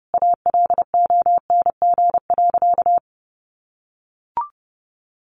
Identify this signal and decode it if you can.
Morse code, heard as a keyed tone, about 30 wpm, 700 Hz. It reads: ALONG.
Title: Sentences from Top 1500 Words 30wpm